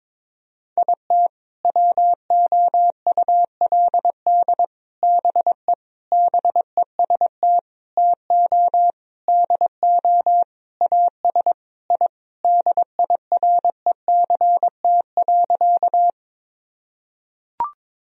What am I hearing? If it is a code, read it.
Morse code, heard as a keyed tone, about 22 wpm, 700 Hz. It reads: IT WOULD BE BEST TO DO AS I DIRECT.
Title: Sentences from Top 1000 Words - Review Part 2 - 22wpm